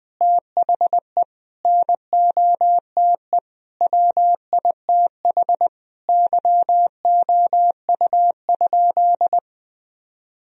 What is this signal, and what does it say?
Morse code, heard as a keyed tone, about 20 wpm, 700 Hz. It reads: THE NOTE WITH YOU?
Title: Sentences from Top 1000 Words - Review Part 2 - 20wpm